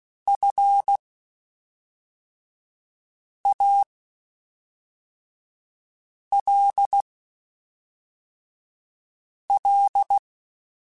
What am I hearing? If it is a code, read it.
Morse code, heard as a keyed tone, about 16 wpm, spaced as for 3 wpm, 800 Hz. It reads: FALL
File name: Quotes Spaced / Quotes Spaced-mp3s / StandFor sp @5